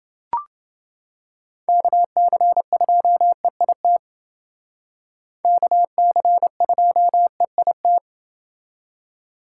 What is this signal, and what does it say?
Morse code, heard as a keyed tone, about 28 wpm, 700 Hz. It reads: KC2EIT KC2EIT